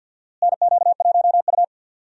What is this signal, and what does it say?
Morse code, heard as a keyed tone, about 52 wpm, 700 Hz. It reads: NQ1U
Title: USA Call Signs - 4 Characters - Repeated 2x - 50wpm